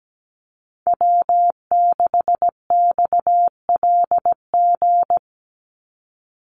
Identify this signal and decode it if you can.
Morse code, heard as a keyed tone, about 17 wpm, 700 Hz. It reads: W6XLG
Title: USA Call Signs 17wpm